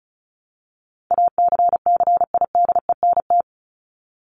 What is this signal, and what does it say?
Morse code, heard as a keyed tone, about 34 wpm, 700 Hz. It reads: ACCIDENT
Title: Top 1751-2000 Words 35wpm